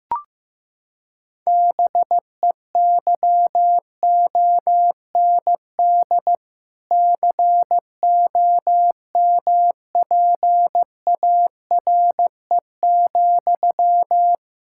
Morse code, heard as a keyed tone, about 15 wpm, 700 Hz. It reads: BEYOND COMPARE,
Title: Sentences from Top 1600 Words 15wpm